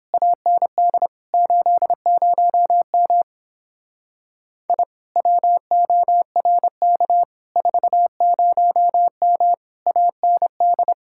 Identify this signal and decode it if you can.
Morse code, heard as a keyed tone, about 31 wpm, 700 Hz. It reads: AND 80M I WORK 40M AND